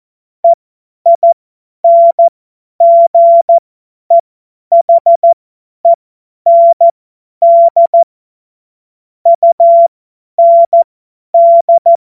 Morse code, heard as a keyed tone, about 14 wpm, spaced as for 10 wpm, 680 Hz. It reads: EINGEHEND UND
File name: Morsekurs / Witze / Witz 12 März 2024 50 F70